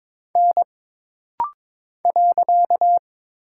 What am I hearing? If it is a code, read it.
Morse code, heard as a keyed tone, about 22 wpm, 700 Hz. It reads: N .
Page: This rapid-fire timing is geared to push your ICR (Instant Character Recognition) skills.